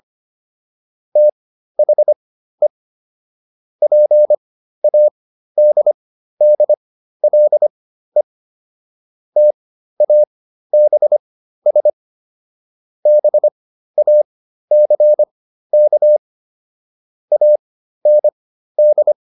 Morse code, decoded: THE PADDLE TABS BACK AND

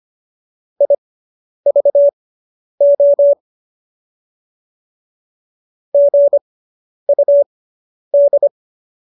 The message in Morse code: IVO GUD

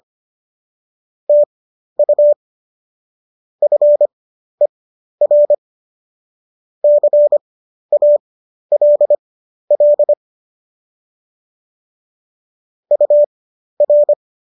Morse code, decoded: TU FER CALL UR

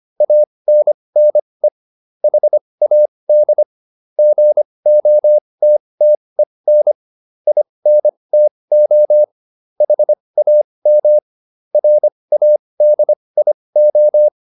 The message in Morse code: ANNE HAD GOTTEN INTO HAM RADIO

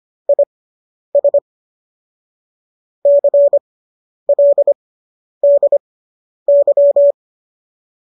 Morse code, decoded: IS CLDY

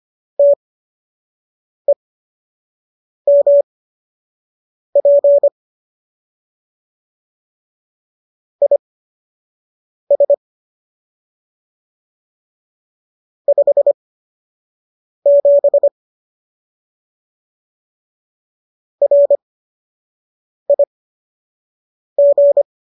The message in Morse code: TEMP IS 57 RIG

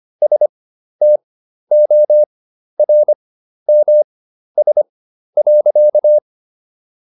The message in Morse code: STORMS.

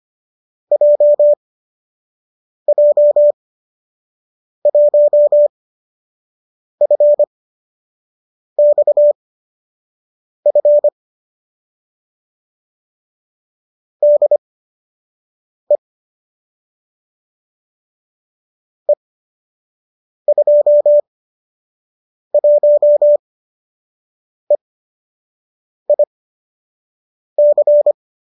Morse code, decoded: JJ1FXF DE E21EIC

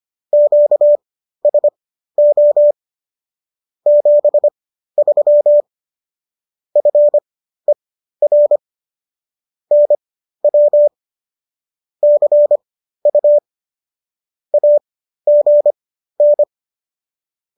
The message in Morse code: QSO 73 FER NW CU AGN

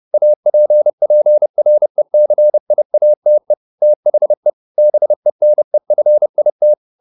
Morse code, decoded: APPRECIATE THE BENEFIT

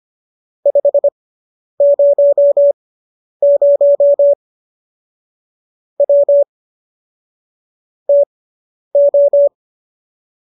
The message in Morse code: 500 W TO

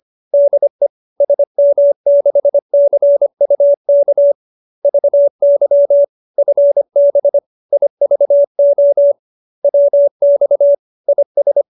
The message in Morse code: DE SM6CUK VY FB IVO WX IS

